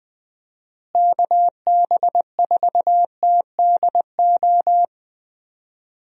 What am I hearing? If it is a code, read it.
Morse code, heard as a keyed tone, about 20 wpm, 700 Hz. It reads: KB4TDO
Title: USA Call Signs - 6 Characters - Repeated 3x - 20wpm